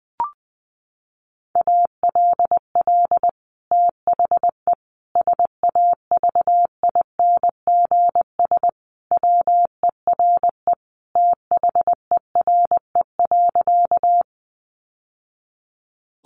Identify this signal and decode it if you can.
Morse code, heard as a keyed tone, about 20 wpm, 700 Hz. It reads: ALL THE SAVINGS WERE THERE.
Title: Sentences from Top 1200 Words 20wpm